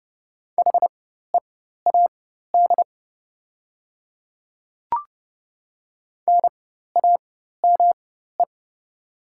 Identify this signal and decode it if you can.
Morse code, heard as a keyed tone, about 29 wpm, spaced as for 14 wpm, 690 Hz. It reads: HEAD NAME
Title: 4x Character Spacing - Top 300 Words - 30wpm